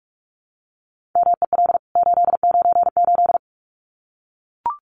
Morse code, heard as a keyed tone, about 44 wpm, 700 Hz. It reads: MEL 898
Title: CWT Exchange 45wpm